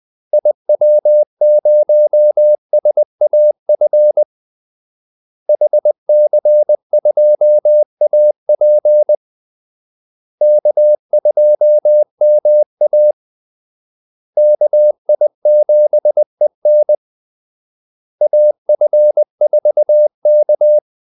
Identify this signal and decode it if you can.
Morse code, heard as a keyed tone, about 20 wpm, 600 Hz. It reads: IW0SAF HC2AP K2MA KI7EN AF4K
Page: Kaldesignaler 20 wpm